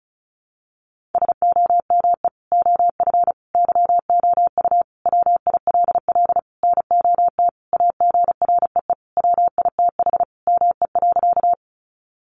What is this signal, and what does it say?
Morse code, heard as a keyed tone, about 34 wpm, 700 Hz. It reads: SOME OF YOU WILL NOT AGREE WITH ME.
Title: Sentences from Top 1100 Words 35wpm